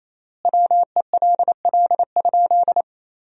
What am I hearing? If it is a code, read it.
Morse code, heard as a keyed tone, about 28 wpm, 700 Hz. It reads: WELL?
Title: 3x Word-Spacing - Sentences from Top 400 Words - 28wpm